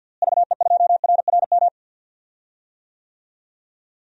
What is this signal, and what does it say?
Morse code, heard as a keyed tone, about 51 wpm, 700 Hz. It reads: VE2RRM
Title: DXCC-100 Call Signs - 6 Characters - Repeated 2x - 50wpm